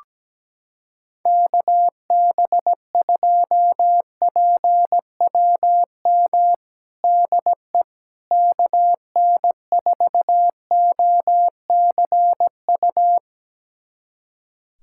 Morse code, heard as a keyed tone, about 17 wpm, 700 Hz. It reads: KB2PWM DE KN4OCU
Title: USA Callsign DE Callsign - 6 Characters - 17wpm